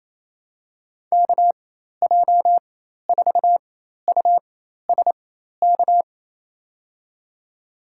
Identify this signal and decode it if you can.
Morse code, heard as a keyed tone, about 28 wpm, spaced as for 13 wpm, 700 Hz. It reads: KJ4USK
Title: USA Call Signs - 6 Characters - 4x Character Spacing - 28wpm